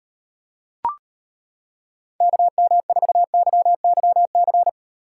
Morse code, heard as a keyed tone, about 38 wpm, 700 Hz. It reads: KM4YYC